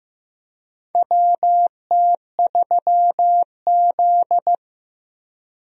Morse code, heard as a keyed tone, about 15 wpm, 700 Hz. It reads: WT3Z
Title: USA Call Signs - 4 Characters - 15wpm